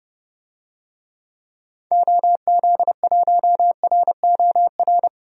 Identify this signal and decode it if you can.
Morse code, heard as a keyed tone, about 30 wpm, 700 Hz. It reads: OZ1ROR
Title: DXCC-100 Call Signs - 6 Characters - 30wpm